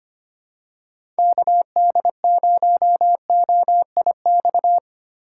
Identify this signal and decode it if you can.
Morse code, heard as a keyed tone, about 25 wpm, 700 Hz. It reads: KD0OIX
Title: K1USN SST QSO Practice - 25wpm